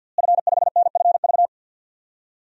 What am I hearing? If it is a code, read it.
Morse code, heard as a keyed tone, about 52 wpm, 700 Hz. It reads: F5NFV
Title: DXCC-100 Call Signs - 5 Characters - Repeated 3x - 50wpm